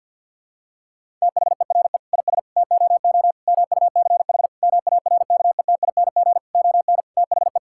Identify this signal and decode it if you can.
Morse code, heard as a keyed tone, about 51 wpm, 700 Hz. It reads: THERE IS TOO MUCH MARKETING ON THE
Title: Sentences from Top 1000 Words - Review Part 5 - 50wpm